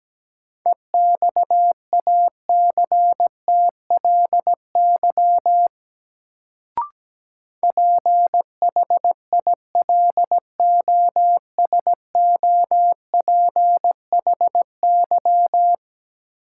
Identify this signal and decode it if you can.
Morse code, heard as a keyed tone, about 17 wpm, 700 Hz. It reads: EXACTLY PHILOSOPHY
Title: Sentences from Top 900 Words - New Word Drill - 17wpm